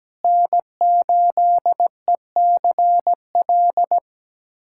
Morse code, heard as a keyed tone, about 17 wpm, 700 Hz. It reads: N8ECL
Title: USA Callsign DE Callsign - 5 Characters - 17wpm